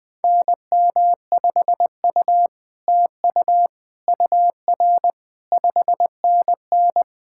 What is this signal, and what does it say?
Morse code, heard as a keyed tone, about 20 wpm, 700 Hz. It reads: NM5U TU UR 5NN
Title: SOTA QSO Practice 20wpm